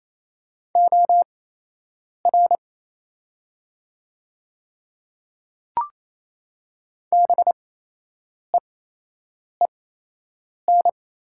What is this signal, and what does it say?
Morse code, heard as a keyed tone, about 28 wpm, spaced as for 8 wpm, 700 Hz. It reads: OR BEEN
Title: 8x Character Spacing - Top 200 Words - 28wpm